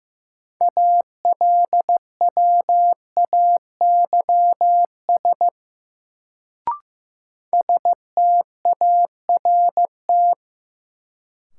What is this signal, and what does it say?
Morse code, heard as a keyed tone, about 15 wpm, 700 Hz. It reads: ALWAYS START
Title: Sentences from Top 300 Words - New Word Drill - 15wpm